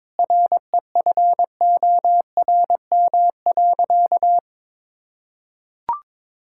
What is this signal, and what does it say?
Morse code, heard as a keyed tone, about 22 wpm, 700 Hz. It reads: REFORM.
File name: Sentences with 7-Word Phrases - 22wpm